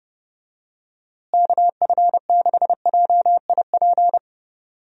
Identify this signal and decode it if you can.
Morse code, heard as a keyed tone, about 31 wpm, 700 Hz. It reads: KF6JIP
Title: USA Call Signs - 6 Characters - Repeated 3x - 30wpm